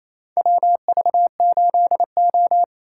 Morse code, heard as a keyed tone, about 28 wpm, 700 Hz. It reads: WV8O
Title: USA Call Signs - 4 Characters - 2x Character Spacing - 28wpm